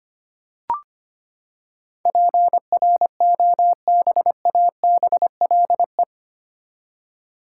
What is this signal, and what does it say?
Morse code, heard as a keyed tone, about 25 wpm, 700 Hz. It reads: PROBABLE
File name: Eight Letter Words - 25wpm